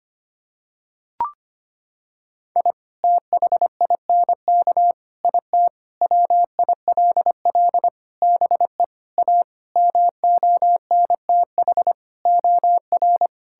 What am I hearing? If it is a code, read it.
Morse code, heard as a keyed tone, about 25 wpm, 700 Hz. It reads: I THINK IT WILL BE A MONTH OR
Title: Sentences from Top 1000 Words - Review Part 5 - 25wpm